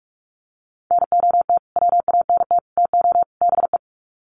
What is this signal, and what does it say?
Morse code, heard as a keyed tone, about 44 wpm, 700 Hz. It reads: NOT WANT TO BE
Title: Sets of 4 Words 45wpm